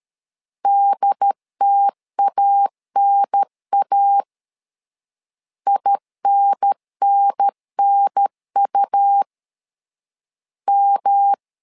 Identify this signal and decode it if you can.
Morse code, heard as a keyed tone, about 13 wpm, 800 Hz. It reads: DTANA INNNU M
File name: LEZIONI / 3 MWRO / 3 M 12wpm 1541